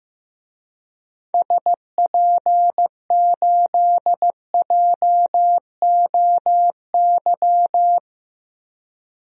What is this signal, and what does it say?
Morse code, heard as a keyed tone, about 15 wpm, 700 Hz. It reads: SP8JOY
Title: DXCC-100 Call Signs - 6 Characters - Repeated 2x - 15wpm